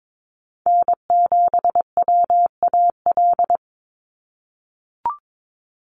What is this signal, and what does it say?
Morse code, heard as a keyed tone, about 22 wpm, 700 Hz. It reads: N7WAL
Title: USA Call Signs 22wpm